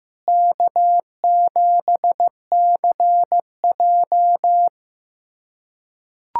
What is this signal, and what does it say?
Morse code, heard as a keyed tone, about 16 wpm, 700 Hz. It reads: K7CJ
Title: USA Callsign DE Callsign - 4 Characters - 15wpm